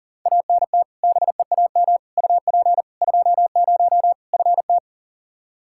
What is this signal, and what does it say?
Morse code, heard as a keyed tone, about 40 wpm, 700 Hz. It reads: ANT BEAM UP 20 FT
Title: Practice QSO - Exchange 2 - Info - 40wpm